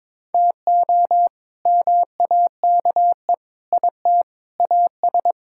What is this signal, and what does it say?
Morse code, heard as a keyed tone, about 21 wpm, 700 Hz. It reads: TO MAKE IT AS
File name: practice / 22wpm / Sets of 8 Words - 22wpm